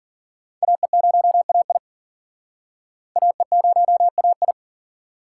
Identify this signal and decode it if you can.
Morse code, heard as a keyed tone, about 52 wpm, 700 Hz. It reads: AE0AI AE0AI